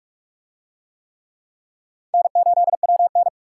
Morse code, heard as a keyed tone, about 39 wpm, 700 Hz. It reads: N8WN